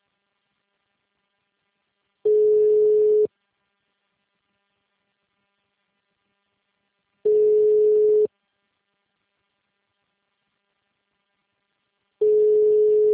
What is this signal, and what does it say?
Ringback tone (Europe, 425 Hz).